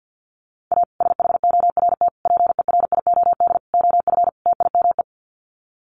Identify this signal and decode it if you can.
Morse code, heard as a keyed tone, about 50 wpm, 700 Hz. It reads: A SHORT PERIOD OF TIME
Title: Sets of 5 Words Encore 50wpm